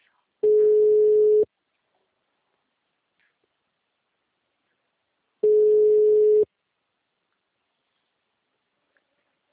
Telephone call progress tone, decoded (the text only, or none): ringback tone